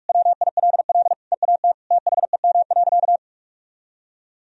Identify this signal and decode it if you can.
Morse code, heard as a keyed tone, about 46 wpm, 700 Hz. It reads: WILL EAT THEM.
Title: Sentences from Top 1000 Words - Review Part 4 - 45wpm